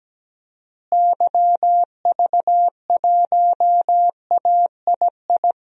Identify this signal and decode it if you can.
Morse code, heard as a keyed tone, about 17 wpm, 700 Hz. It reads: YV1AII
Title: DXCC-100 Call Signs - 6 Characters - Repeated 2x - 17wpm